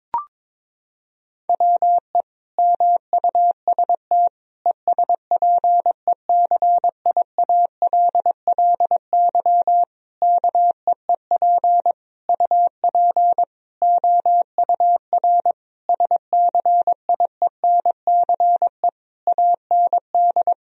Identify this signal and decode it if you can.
Morse code, heard as a keyed tone, about 22 wpm, 700 Hz. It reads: WE MUST ESPECIALLY KEEP UP OUR SCIENCE AND
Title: Sentences from Top 1000 Words - Review Part 4 - 22wpm